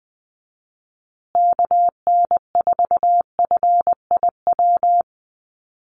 Morse code, heard as a keyed tone, about 20 wpm, 700 Hz. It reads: KN4FIW
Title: USA Call Signs Repeated 2x 20wpm